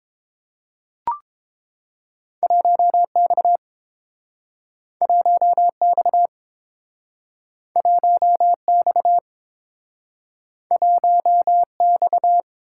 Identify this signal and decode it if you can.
Morse code, heard as a keyed tone, about 30 wpm, 700 Hz. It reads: 1X 1X 1X 1X